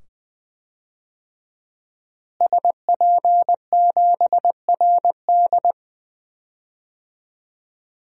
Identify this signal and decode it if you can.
Morse code, heard as a keyed tone, about 20 wpm, 700 Hz. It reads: SP7RD
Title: DXCC-100 Call Signs - 5 Characters - Repeated 2x - 20wpm